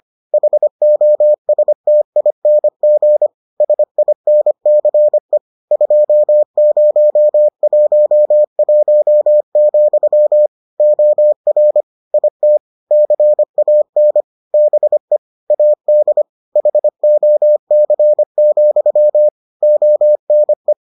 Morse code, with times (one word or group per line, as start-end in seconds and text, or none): 0.3-3.3 HOSTING
3.6-5.4 SINCE
5.7-10.5 2011,
10.8-11.8 OR
12.1-12.6 IT
12.9-14.2 CAN
14.5-15.2 BE
15.5-16.2 AD
16.6-19.3 HOC,
19.6-20.7 ONE